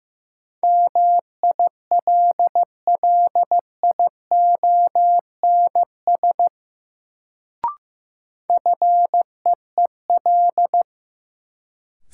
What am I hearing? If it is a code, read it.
Morse code, heard as a keyed tone, about 15 wpm, 700 Hz. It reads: MILLIONS FEEL
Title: Sets of 5 Words - New Word Drill - 15wpm